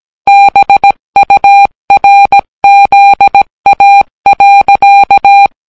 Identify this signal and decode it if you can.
Morse code, heard as a keyed tone, about 17 wpm, 800 Hz. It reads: BURZA.